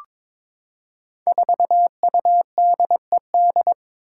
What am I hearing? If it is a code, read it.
Morse code, heard as a keyed tone, about 22 wpm, 700 Hz. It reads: 4UDED
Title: Five Challenging Letter-Number Combinations 22wpm